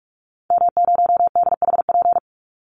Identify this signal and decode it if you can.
Morse code, heard as a keyed tone, about 44 wpm, 700 Hz. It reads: M0DHP
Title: Call Signs - 45wpm